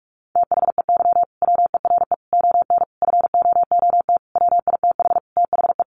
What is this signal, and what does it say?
Morse code, heard as a keyed tone, about 44 wpm, 700 Hz. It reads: THEY WERE ON FOOT WITH THE
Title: Sentences from Top 1500 Words 45wpm